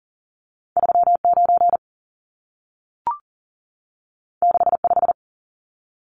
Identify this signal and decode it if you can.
Morse code, heard as a keyed tone, about 40 wpm, 700 Hz. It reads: 39 65
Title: Numbers - 2-Digit 40wpm